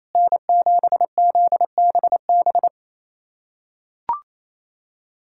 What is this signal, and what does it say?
Morse code, heard as a keyed tone, about 28 wpm, 700 Hz. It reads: N7ZBB
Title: USA Callsign DE Callsign - 5 Characters - 28wpm